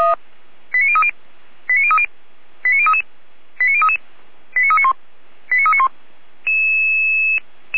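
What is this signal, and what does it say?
Five-tone selective call (ZVEI), heard as quick runs of five tones, tones of about 70 ms. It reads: 89030 89030 89033 89033 89391 89391
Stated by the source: ZVEI five-tone selective call received by SDR.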